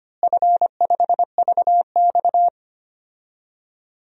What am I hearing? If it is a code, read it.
Morse code, heard as a keyed tone, about 25 wpm, 700 Hz. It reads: F5VX
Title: DXCC-100 Call Signs - 4 Characters - Repeated 2x - 25wpm